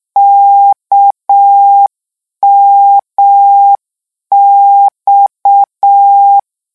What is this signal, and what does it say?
Morse code, heard as a keyed tone, about 6 wpm, 800 Hz. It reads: KMX
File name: morse mp3 / lecon 1-vitesse 6